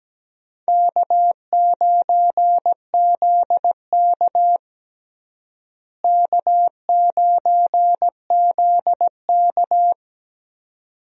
Morse code, heard as a keyed tone, about 17 wpm, 700 Hz. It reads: K9ZK K9ZK
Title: USA Call Signs - 4 Characters - Repeated 2x - 17wpm